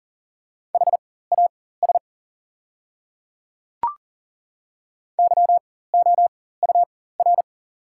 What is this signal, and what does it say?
Morse code, heard as a keyed tone, about 40 wpm, spaced as for 19 wpm, 690 Hz. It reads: HAS YOUR
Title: 4x Character Spacing - Top 100 Words - 40wpm